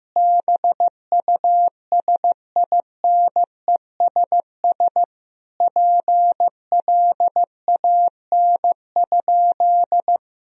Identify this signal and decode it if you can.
Morse code, heard as a keyed tone, about 15 wpm, 700 Hz. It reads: BUSINESS PLAN?
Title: Sentences from Top 1000 Words - Review Part 3 - 15wpm